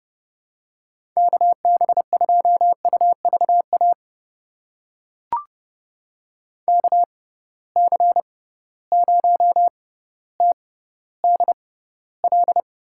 Morse code, heard as a keyed tone, about 31 wpm, spaced as for 16 wpm, 700 Hz. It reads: KB2UVA KC0TDL
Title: USA Call Signs - 6 Characters - 6x Character Spacing - 30wpm